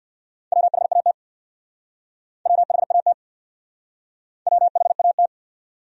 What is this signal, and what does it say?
Morse code, heard as a keyed tone, about 58 wpm, 700 Hz. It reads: WHAT WHAT WHAT
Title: Speed-Racing Top 500 Words 50wpm